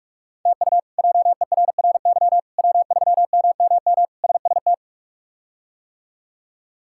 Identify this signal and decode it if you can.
Morse code, heard as a keyed tone, about 46 wpm, 700 Hz. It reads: TU JERRY W3MMM SST
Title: K1USN SST QSO Practice - 45wpm